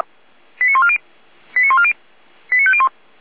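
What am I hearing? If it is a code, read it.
Five-tone selective call (ZVEI), heard as quick runs of five tones, tones of about 70 ms. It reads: 89140 89140 89501
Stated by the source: ZVEI five-tone call received on SDR, channel 455.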